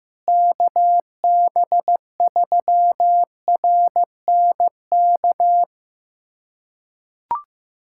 Morse code, heard as a keyed tone, about 15 wpm, 700 Hz. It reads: KB3RNK
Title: USA Callsign DE Callsign 15wpm